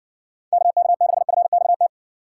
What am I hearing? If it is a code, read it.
Morse code, heard as a keyed tone, about 60 wpm, 700 Hz. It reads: KK6FXT